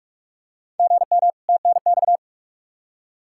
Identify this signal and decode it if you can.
Morse code, heard as a keyed tone, about 46 wpm, 700 Hz. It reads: GM TNX
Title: Practice QSO - Exchange 1 - Beginning - 45wpm